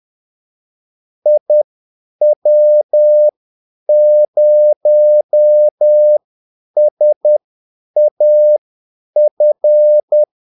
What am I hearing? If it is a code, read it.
Morse code, heard as a keyed tone, about 10 wpm, spaced as for 8 wpm, 600 Hz. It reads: IW0SAF
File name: Callsigns-10-8wpm